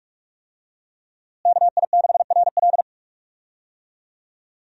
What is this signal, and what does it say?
Morse code, heard as a keyed tone, about 42 wpm, 700 Hz. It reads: KI6RL
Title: USA Call Signs - 5 Characters - Repeated 2x - 45wpm